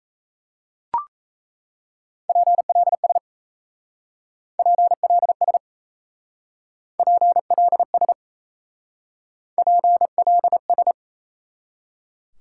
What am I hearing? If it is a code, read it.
Morse code, heard as a keyed tone, about 34 wpm, 700 Hz. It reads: PLS PLS PLS PLS
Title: Speed-Racing Top 90 QSO Elements 28wpm